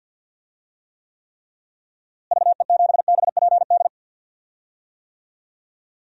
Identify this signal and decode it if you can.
Morse code, heard as a keyed tone, about 51 wpm, 700 Hz. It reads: VE7BPD
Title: DXCC-100 Call Signs - 6 Characters - Repeated 3x - 50wpm